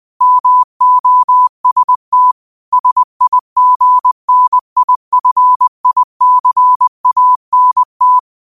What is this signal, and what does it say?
Morse code, heard as a keyed tone, about 20 wpm, 1000 Hz. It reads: MOST SIGNIFICANT
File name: qotdfiles / 20wpm